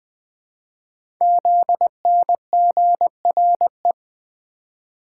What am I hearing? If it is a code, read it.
Morse code, heard as a keyed tone, about 20 wpm, 700 Hz. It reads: ZNGRE
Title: Code Groups - 5 Letters - 20wpm